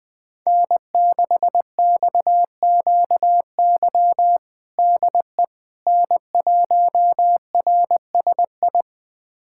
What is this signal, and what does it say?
Morse code, heard as a keyed tone, about 20 wpm, 700 Hz. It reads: N6XQY DE N1RSI
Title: USA Callsign DE Callsign - 5 Characters - 20wpm